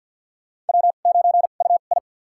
Morse code, heard as a keyed tone, about 52 wpm, spaced as for 36 wpm, 700 Hz. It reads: W9UI
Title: USA Call Signs - 4 Characters - 2x Character Spacing - 50wpm